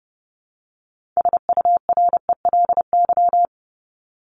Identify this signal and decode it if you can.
Morse code, heard as a keyed tone, about 29 wpm, 700 Hz. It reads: SURELY
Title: LY Suffix - Top 100 Words with Base Words 30wpm